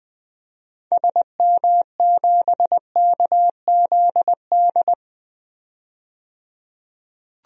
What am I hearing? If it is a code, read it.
Morse code, heard as a keyed tone, about 20 wpm, 700 Hz. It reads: SM7KZD
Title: DXCC-100 Call Signs - 6 Characters - 20wpm